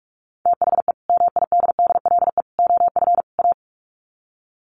Morse code, heard as a keyed tone, about 44 wpm, 700 Hz. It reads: THE MIDDLE OF A
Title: Sets of 4 Words 45wpm